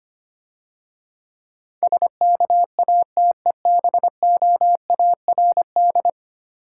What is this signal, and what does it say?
Morse code, heard as a keyed tone, about 25 wpm, 700 Hz. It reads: SKATEBOARD
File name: Easy Words - Speed-Racing with Hints - 25wpm